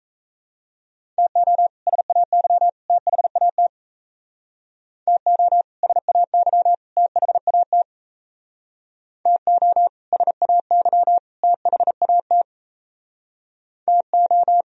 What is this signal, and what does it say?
Morse code, heard as a keyed tone, about 34 wpm, 700 Hz. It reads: TO SAY THAT TO SAY THAT TO SAY THAT TO
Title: Speed-Racing - Sets of 3 Words 28wpm